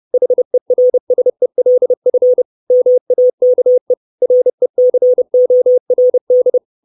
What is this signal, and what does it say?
Morse code, heard as a keyed tone, about 30 wpm, 500 Hz. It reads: HERSELF MAKE RECORD
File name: 632 Words 30WPM